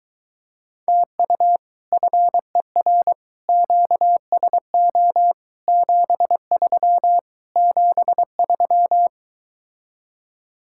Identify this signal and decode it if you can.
Morse code, heard as a keyed tone, about 23 wpm, 700 Hz. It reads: TU FER QSO 73 73